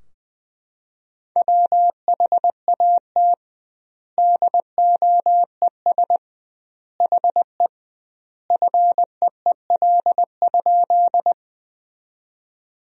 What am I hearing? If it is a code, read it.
Morse code, heard as a keyed tone, about 21 wpm, 700 Hz. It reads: WHAT DOES HE FEEL?
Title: 2x Word-Spacing - Sentences from Top 400 Words - 20wpm